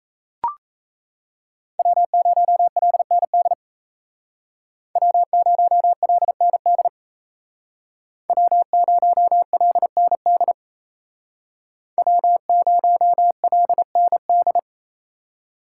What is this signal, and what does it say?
Morse code, heard as a keyed tone, about 36 wpm, 700 Hz. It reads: W0LND W0LND W0LND W0LND